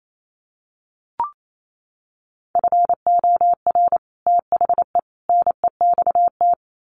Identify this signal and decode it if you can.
Morse code, heard as a keyed tone, about 28 wpm, 700 Hz. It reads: FOR THE NEXT
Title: Sets of 3 Words 28wpm